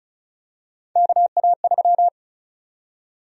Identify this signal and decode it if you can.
Morse code, heard as a keyed tone, about 34 wpm, 700 Hz. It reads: KA3
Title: USA Call Sign Prefixes - 35wpm